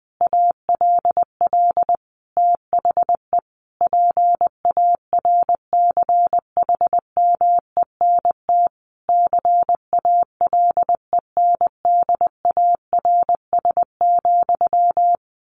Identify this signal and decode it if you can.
Morse code, heard as a keyed tone, about 20 wpm, 700 Hz. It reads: ALL THE PARCHMENT CALENDARS,
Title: A Sound Of Thunder 20wpm